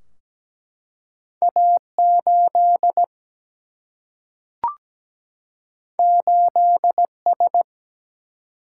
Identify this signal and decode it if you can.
Morse code, heard as a keyed tone, about 17 wpm, 700 Hz. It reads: A 8 8 S
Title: Single-Letter Single-Number 17wpm